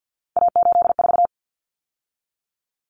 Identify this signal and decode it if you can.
Morse code, heard as a keyed tone, about 50 wpm, 700 Hz. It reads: A84
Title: Common Cut Numbers - 3-Digit 50wpm